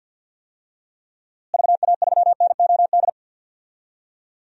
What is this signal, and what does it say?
Morse code, heard as a keyed tone, about 46 wpm, 700 Hz. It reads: VA3NOD
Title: DXCC-100 Call Signs - 6 Characters - Repeated 2x - 50wpm